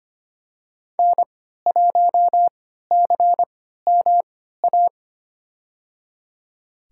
Morse code, heard as a keyed tone, about 25 wpm, spaced as for 14 wpm, 700 Hz. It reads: N1CMA
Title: USA Call Signs - 5 Characters - 3x Character Spacing - 25wpm